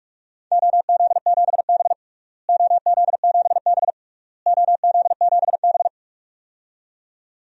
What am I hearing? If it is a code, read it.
Morse code, heard as a keyed tone, about 46 wpm, 700 Hz. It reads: OZ7B OZ7B OZ7B
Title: DXCC-100 Call Signs - 4 Characters - Repeated 3x - 45wpm